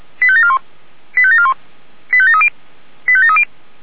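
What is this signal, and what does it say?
Five-tone selective call (ZVEI), heard as quick runs of five tones, tones of about 70 ms. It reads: 85741 85741 85730 85730
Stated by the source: ZVEI five-tone selective call received with an SDR.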